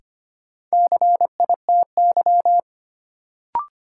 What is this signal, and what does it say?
Morse code, heard as a keyed tone, about 25 wpm, 700 Hz. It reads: CITY
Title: Sentences from Top 300 Words - New Word Drill - 25wpm